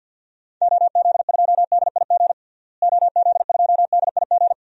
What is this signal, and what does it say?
Morse code, heard as a keyed tone, about 53 wpm, 700 Hz. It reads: OZ2DIG OZ2DIG
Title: DXCC-100 Call Signs - 6 Characters - Repeated 2x - 50wpm